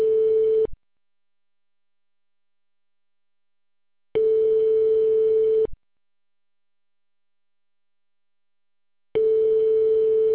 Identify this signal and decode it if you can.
Ringback tone (France), 440 Hz.